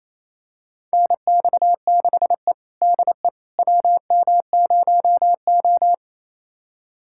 Morse code, heard as a keyed tone, about 28 wpm, 700 Hz. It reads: NX6E DE WM0O
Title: USA Callsign DE Callsign - 4 Characters - 28wpm